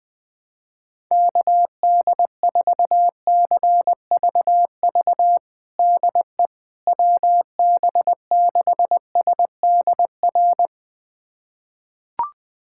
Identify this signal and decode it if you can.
Morse code, heard as a keyed tone, about 21 wpm, 700 Hz. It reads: KD4CVV DE WB6SDR
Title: USA Callsign DE Callsign - 6 Characters - 20wpm